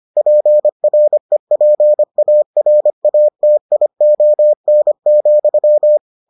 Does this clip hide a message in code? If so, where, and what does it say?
Yes: Morse code from 0.2 to 6.0 s, heard as a keyed tone, about 25 wpm, 600 Hz. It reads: PREPARATION,